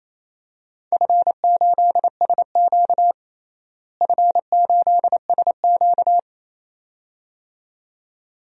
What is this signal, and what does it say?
Morse code, heard as a keyed tone, about 28 wpm, 700 Hz. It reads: F8SQ F8SQ
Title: DXCC-100 Call Signs - 4 Characters - Repeated 2x - 28wpm